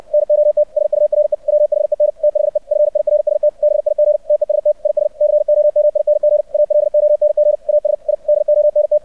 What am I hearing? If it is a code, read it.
Morse code, heard as a keyed tone, about 64 wpm, 600 Hz. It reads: NOT SURE WHEN THE FIRST BIG TEST IS COMING ALONG AS I DONT
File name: archived / CRSnet